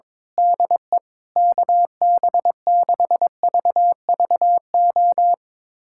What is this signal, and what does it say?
Morse code, heard as a keyed tone, about 22 wpm, 700 Hz. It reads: DE KB6VVO
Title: USA Callsign DE Callsign 22wpm